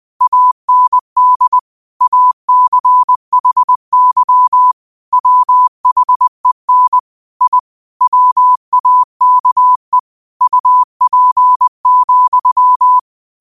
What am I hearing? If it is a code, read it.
Morse code, heard as a keyed tone, about 20 wpm, 1000 Hz. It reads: AND ACHY WHEN I WAKE UP,
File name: qotdfiles / 20wpm